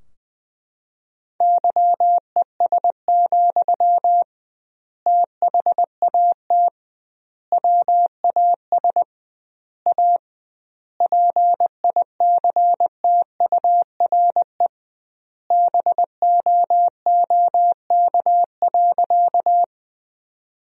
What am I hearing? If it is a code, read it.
Morse code, heard as a keyed tone, about 21 wpm, 700 Hz. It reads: YES, THAT WAS A PICTURE BOOK.
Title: 2x Word-Spacing - Sentences from Top 500 Words - 20wpm